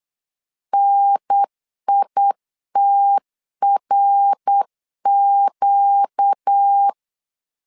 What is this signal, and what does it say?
Morse code, heard as a keyed tone, about 8 wpm, 800 Hz. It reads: NITRQ
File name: 8wpm